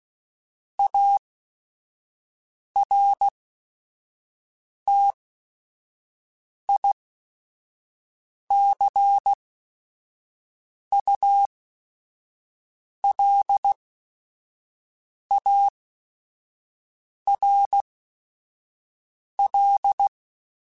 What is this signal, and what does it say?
Morse code, heard as a keyed tone, about 16 wpm, spaced as for 5 wpm, 800 Hz. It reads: ARTICULARL
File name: Technical Subjects / Technical Subjects-mp3s / Reactance @5